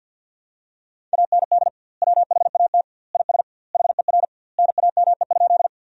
Morse code, heard as a keyed tone, about 46 wpm, 700 Hz. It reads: AND WHAT IS HER NAME?
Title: 2x Word-Spacing - Sentences from Top 200 Words - 50wpm